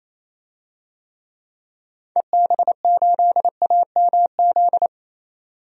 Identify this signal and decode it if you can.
Morse code, heard as a keyed tone, about 28 wpm, 700 Hz. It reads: EB8AMZ
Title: DXCC-100 Call Signs - 6 Characters - 28wpm